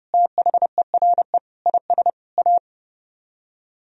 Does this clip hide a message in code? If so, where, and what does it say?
Yes: Morse code from 0.1 to 2.6 s, heard as a keyed tone, about 38 wpm, 700 Hz. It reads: THERE IS A